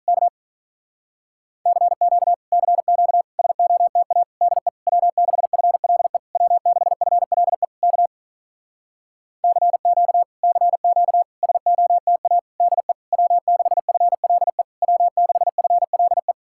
Morse code, heard as a keyed tone, about 43 wpm, 700 Hz. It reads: K CQ CQ SOTA DE W6FLE W6FLE K CQ CQ SOTA DE W6FLE W6FLE
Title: Speed-Racing Common QSO Phrases - 35wpm